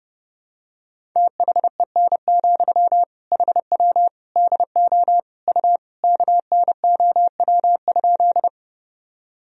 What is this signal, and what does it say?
Morse code, heard as a keyed tone, about 32 wpm, 700 Hz. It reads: THEN, HW DO U KNOW?
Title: Sentences from Top 100 Words and Abbreviations 30wpm